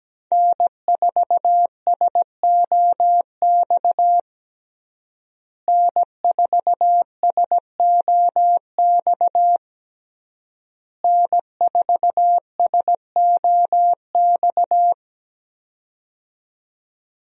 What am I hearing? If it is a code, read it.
Morse code, heard as a keyed tone, about 17 wpm, 700 Hz. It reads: N4SOX N4SOX N4SOX
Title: USA Call Signs - 5 Characters - Repeated 3x - 17wpm